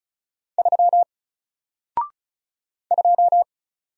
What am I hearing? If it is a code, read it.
Morse code, heard as a keyed tone, about 35 wpm, 700 Hz. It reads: 3 2
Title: Single Character (Letter:Number:Punctuation:Prosign) 35wpm